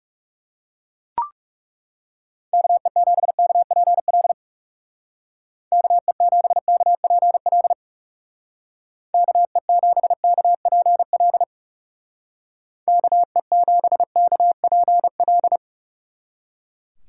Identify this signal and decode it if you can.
Morse code, heard as a keyed tone, about 36 wpm, 700 Hz. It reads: KE7KPL KE7KPL KE7KPL KE7KPL
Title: Speed Racing - USA Call Signs 30wpm